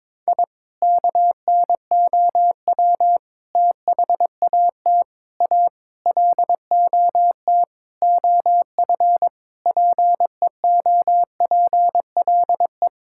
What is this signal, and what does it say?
Morse code, heard as a keyed tone, about 22 wpm, 700 Hz. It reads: I KNOW THAT A LOT OF PEOPLE
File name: practice / 22wpm / Sentences with 4-Word Phrases - 22wpm